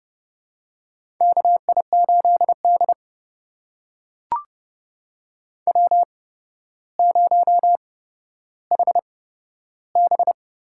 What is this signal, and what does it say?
Morse code, heard as a keyed tone, about 30 wpm, spaced as for 13 wpm, 700 Hz. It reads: KI8D W0HB
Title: USA Call Signs - 4 Characters - 8x Character Spacing - 30wpm